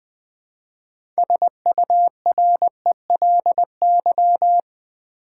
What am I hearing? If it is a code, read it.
Morse code, heard as a keyed tone, about 20 wpm, 700 Hz. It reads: SURELY
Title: 6 Letter Words - First 3 Letters Given - 20wpm